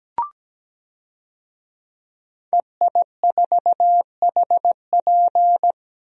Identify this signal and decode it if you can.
Morse code, heard as a keyed tone, about 17 wpm, 700 Hz. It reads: EI4HP
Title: DXCC-100 Call Signs - 5 Characters - 17wpm